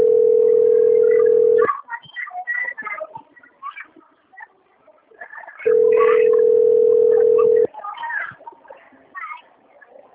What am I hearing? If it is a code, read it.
Ringback tone (North America), 440 + 480 Hz.